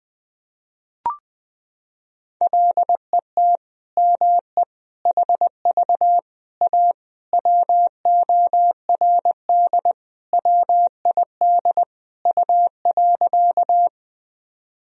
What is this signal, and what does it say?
Morse code, heard as a keyed tone, about 21 wpm, 700 Hz. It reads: LET ME HV A WORD WID U.
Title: Sentences from Top 300 Words and Abbreviations 20wpm